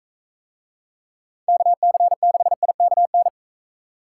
Morse code, heard as a keyed tone, about 40 wpm, 700 Hz. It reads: KC6IKN